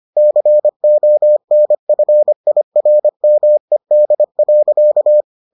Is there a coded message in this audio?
Yes: Morse code from 0.2 to 5.2 s, heard as a keyed tone, about 25 wpm, 600 Hz. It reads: CONFIRMED.